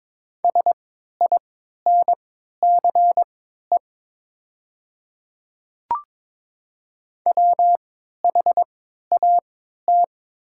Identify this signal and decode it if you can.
Morse code, heard as a keyed tone, about 21 wpm, spaced as for 12 wpm, 700 Hz. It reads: SINCE WHAT
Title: 3x Character Spacing - Top 300 Words - 22wpm